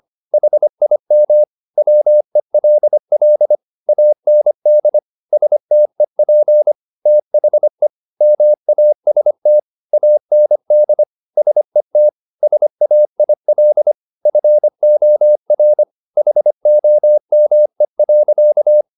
Morse code, heard as a keyed tone, about 25 wpm, 600 Hz. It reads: HIM WELL AND STEP THE MAST AND SET SAIL FOR HOME.